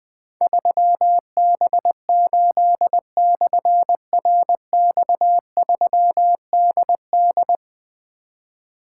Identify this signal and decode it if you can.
Morse code, heard as a keyed tone, about 20 wpm, 700 Hz. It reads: 3B8/RX3DD
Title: Difficult Call Signs 20wpm